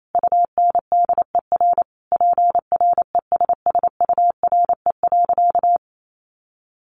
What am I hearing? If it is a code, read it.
Morse code, heard as a keyed tone, about 28 wpm, 700 Hz. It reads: UNDER PRESSURE.
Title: Sentences from Top 1400 Words 28wpm